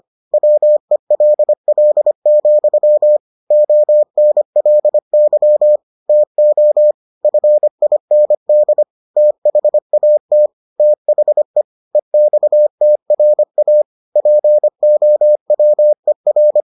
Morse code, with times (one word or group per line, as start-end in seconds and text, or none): 0.3-3.2 WELL,
3.5-5.8 ONLY
6.1-6.9 TO
7.2-8.8 FIND
9.2-10.5 THAT
10.8-11.6 THE
11.9-13.8 EXTRA
14.2-16.6 POWER